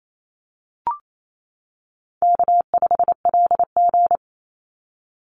Morse code, heard as a keyed tone, about 28 wpm, 700 Hz. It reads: K5LG
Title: Call Signs - 28wpm